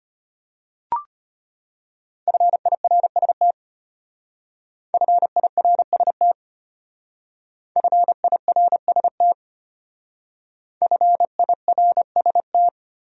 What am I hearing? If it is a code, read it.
Morse code, heard as a keyed tone, about 31 wpm, 700 Hz. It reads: FIRST FIRST FIRST FIRST
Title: Speed-Racing Top 200 Words 25wpm